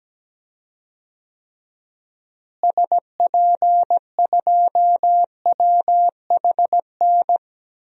Morse code, heard as a keyed tone, about 17 wpm, 700 Hz. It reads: SP2WHN
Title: DXCC-100 Call Signs - 6 Characters - Repeated 2x - 17wpm